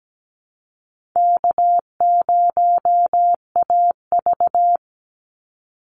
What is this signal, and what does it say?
Morse code, heard as a keyed tone, about 17 wpm, 700 Hz. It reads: K0AV
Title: Easy Call Signs - 17wpm